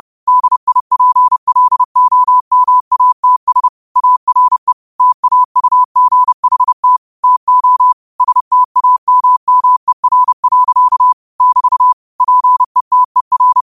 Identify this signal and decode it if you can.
Morse code, heard as a keyed tone, about 30 wpm, 1000 Hz. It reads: DIPLOMATS ARE TAUGHT TO STAMMER. <BT> PETER
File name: qotdfiles / 30wpm